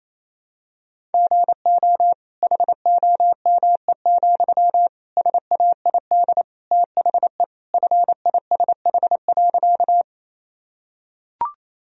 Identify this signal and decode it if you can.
Morse code, heard as a keyed tone, about 28 wpm, 700 Hz. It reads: GO HOME, SAID THE FISH.
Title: Sentences from Top 1000 Words - Review Part 1 - 28wpm